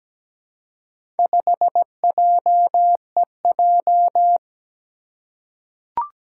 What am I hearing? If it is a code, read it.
Morse code, heard as a keyed tone, about 17 wpm, 700 Hz. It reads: 5JEJ
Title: Four Challenging Letter-Number Combinations 17wpm